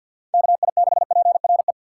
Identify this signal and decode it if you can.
Morse code, heard as a keyed tone, about 48 wpm, 700 Hz. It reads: KI6PRE
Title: USA Call Signs - 6 Characters - 3x Character Spacing - 50wpm